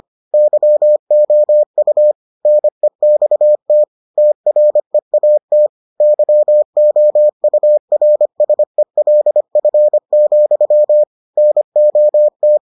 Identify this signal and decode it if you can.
Morse code, heard as a keyed tone, about 25 wpm, 600 Hz. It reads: YOU NEXT TREAT YOURSELF, NOT